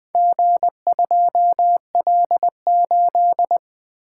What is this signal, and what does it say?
Morse code, heard as a keyed tone, about 20 wpm, 700 Hz. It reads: G2L8
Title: Four Letter-Number Combinations - 20wpm